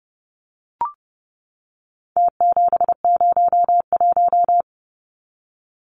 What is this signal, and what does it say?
Morse code, heard as a keyed tone, about 29 wpm, 700 Hz. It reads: T701
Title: Numbers - Contest Style - 4-Digit 30wpm